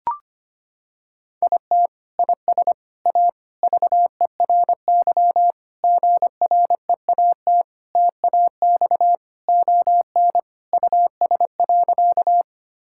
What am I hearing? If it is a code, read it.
Morse code, heard as a keyed tone, about 25 wpm, 700 Hz. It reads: IT IS A VERY GREAT TAX ON US.
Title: Sentences from Top 1000 Words - Review Part 5 - 25wpm